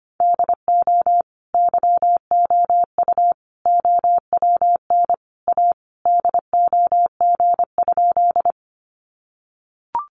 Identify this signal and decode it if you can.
Morse code, heard as a keyed tone, about 25 wpm, 700 Hz. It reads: DO YOU OWN A DOG?
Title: Sentences from Top 400 Words 25wpm